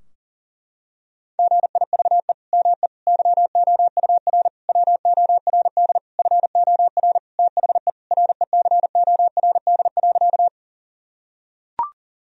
Morse code, heard as a keyed tone, about 40 wpm, 700 Hz. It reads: GIVE ME YOUR WORD FOR THE RECORD.
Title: Sentences from Top 1000 Words - Review Part 3 - 40wpm